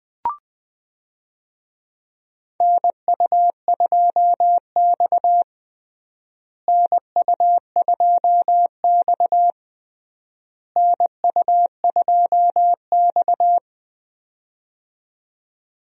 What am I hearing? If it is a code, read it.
Morse code, heard as a keyed tone, about 20 wpm, 700 Hz. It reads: NU2X NU2X NU2X
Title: USA Call Signs - 4 Characters - Repeated 3x - 20wpm